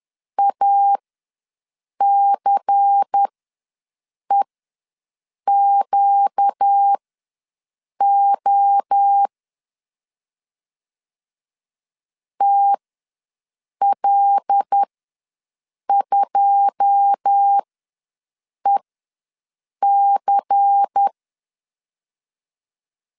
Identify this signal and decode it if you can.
Morse code, heard as a keyed tone, about 10 wpm, spaced as for 6 wpm, 800 Hz. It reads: ACEQO TL2EC
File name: LEZIONI / 9 2 ws 10wpm 4130